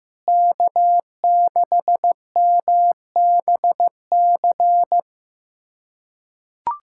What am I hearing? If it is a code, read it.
Morse code, heard as a keyed tone, about 15 wpm, 700 Hz. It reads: K6MBC
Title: USA Callsign DE Callsign - 5 Characters - 15wpm